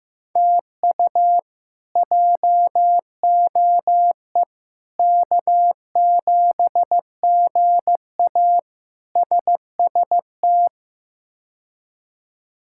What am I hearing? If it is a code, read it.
Morse code, heard as a keyed tone, about 15 wpm, 700 Hz. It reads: TU JOE K7GA SST
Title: K1USN SST QSO Practice - 15wpm